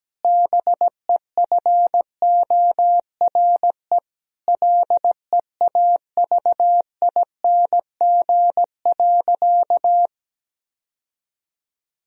Morse code, heard as a keyed tone, about 17 wpm, 700 Hz. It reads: BEFORE LEAVING.
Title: Sentences from Top 1000 Words - Review Part 4 - 17wpm